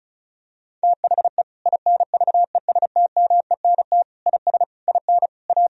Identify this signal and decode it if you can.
Morse code, heard as a keyed tone, about 36 wpm, 700 Hz. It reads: THE INVESTMENT IS IN A
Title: Sentences from Top 1000 Words - Review Part 4 - 35wpm